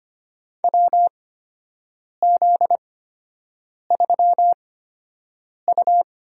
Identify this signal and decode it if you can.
Morse code, heard as a keyed tone, about 25 wpm, spaced as for 7 wpm, 700 Hz. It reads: WZ3U
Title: USA Call Signs - 4 Characters - 8x Character Spacing - 25wpm